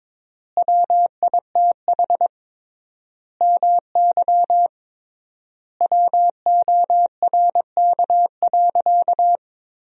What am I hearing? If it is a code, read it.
Morse code, heard as a keyed tone, about 22 wpm, 700 Hz. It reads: WITH MY WORK.
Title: 3x Word-Spacing - Sentences from Top 300 Words - 22wpm